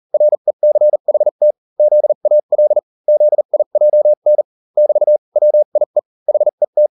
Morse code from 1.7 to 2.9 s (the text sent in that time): ZAL